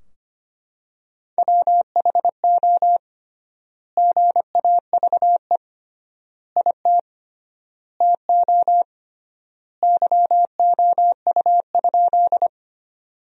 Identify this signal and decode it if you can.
Morse code, heard as a keyed tone, about 25 wpm, 700 Hz. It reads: WHO GAVE IT TO YOU?
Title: 3x Word-Spacing - Sentences from Top 500 Words - 25wpm